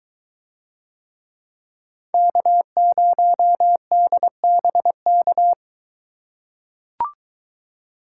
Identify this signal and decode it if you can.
Morse code, heard as a keyed tone, about 21 wpm, 700 Hz. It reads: K0DBK